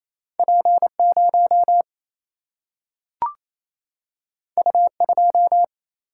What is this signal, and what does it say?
Morse code, heard as a keyed tone, about 28 wpm, 700 Hz. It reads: P 0 U 2
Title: Single-Letter Single-Number 28wpm